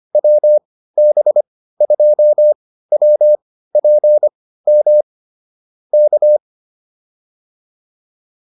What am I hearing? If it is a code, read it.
Morse code, heard as a keyed tone, about 25 wpm, spaced as for 15 wpm, 600 Hz. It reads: WB2WPM K
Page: intermediate-practice-files